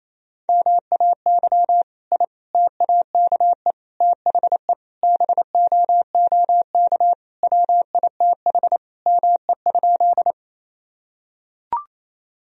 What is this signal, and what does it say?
Morse code, heard as a keyed tone, about 28 wpm, 700 Hz. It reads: MAY I TAKE THE BOOK WITH ME?
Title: Sentences from Top 1000 Words - Review Part 1 - 28wpm